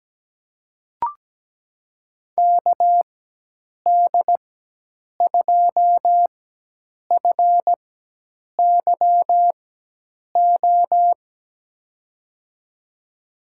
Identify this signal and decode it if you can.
Morse code, heard as a keyed tone, about 17 wpm, spaced as for 8 wpm, 700 Hz. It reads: KD2FYO
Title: USA Call Signs - 4x Character Spacing - 17wpm